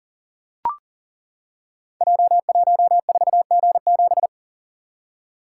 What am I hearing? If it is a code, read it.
Morse code, heard as a keyed tone, about 40 wpm, 700 Hz. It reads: J14G7
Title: Code Groups - 5 Letter:Numbers - 40wpm